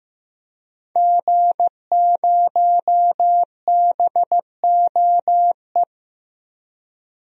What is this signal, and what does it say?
Morse code, heard as a keyed tone, about 15 wpm, 700 Hz. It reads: G0BOE
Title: DXCC-100 Call Signs - 5 Characters - Repeated 2x - 15wpm